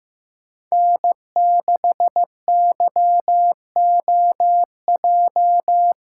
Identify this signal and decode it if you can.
Morse code, heard as a keyed tone, about 15 wpm, 700 Hz. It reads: N6YOJ
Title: USA Callsign DE Callsign - 5 Characters - 15wpm